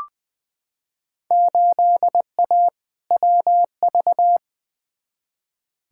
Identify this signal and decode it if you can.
Morse code, heard as a keyed tone, about 20 wpm, 700 Hz. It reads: 8A WV
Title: Field Day Exchanges Encore - 20wpm